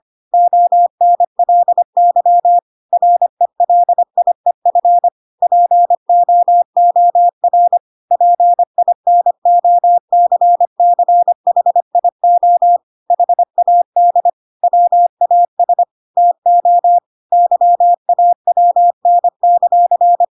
Morse code, heard as a keyed tone, about 25 wpm, 700 Hz. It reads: ONLY RELIEF POOR PINOCCHIO HAD WAS TO YAWN;
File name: Pinocchio / Pinocchio ch5 25 25 ik1hge